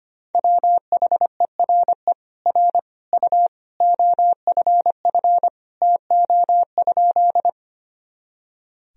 Morse code, heard as a keyed tone, about 25 wpm, 700 Hz. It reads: WHERE R U OFF TO?
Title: Sentences from Top 200 Words and Abbreviations 25wpm